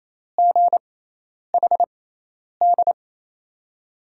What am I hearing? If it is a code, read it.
Morse code, heard as a keyed tone, about 27 wpm, spaced as for 10 wpm, 700 Hz. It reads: GHD
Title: USA Call Signs - 5 Characters - 6x Character Spacing - 28wpm